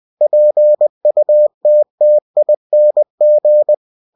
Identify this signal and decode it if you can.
Morse code, heard as a keyed tone, about 20 wpm, 600 Hz. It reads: PUTTING
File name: cw-books / winnie-the-pooh / winnie-20wpm-page-0043